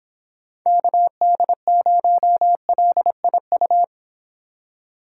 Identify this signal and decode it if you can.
Morse code, heard as a keyed tone, about 29 wpm, 700 Hz. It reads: KD0LIU